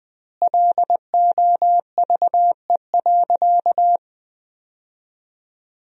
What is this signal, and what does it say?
Morse code, heard as a keyed tone, about 20 wpm, 700 Hz. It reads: LOVE.
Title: 2x Word-Spacing - Sentences from Top 400 Words - 20wpm